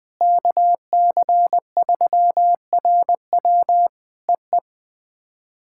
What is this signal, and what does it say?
Morse code, heard as a keyed tone, about 21 wpm, 700 Hz. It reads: KC3RW EE
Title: POTA QSO Practice - 20wpm